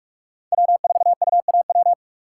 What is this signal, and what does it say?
Morse code, heard as a keyed tone, about 46 wpm, 700 Hz. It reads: W4UAW
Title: USA Call Signs - 3x Character Spacing - 45wpm